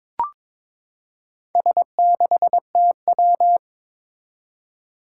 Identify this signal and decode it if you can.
Morse code, heard as a keyed tone, about 22 wpm, 700 Hz. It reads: S6TW
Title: Four Challenging Letter-Number Combinations 22wpm